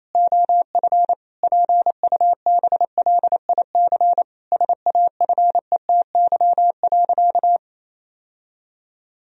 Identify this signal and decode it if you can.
Morse code, heard as a keyed tone, about 28 wpm, 700 Hz. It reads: OF PUBLIC SAFETY.
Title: Sentences from Top 1000 Words - Review Part 4 - 28wpm